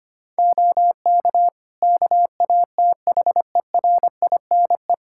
Morse code, heard as a keyed tone, about 25 wpm, 700 Hz. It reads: OK KATHERINE
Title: Practice QSO - Exchange 3 - More Info - 25wpm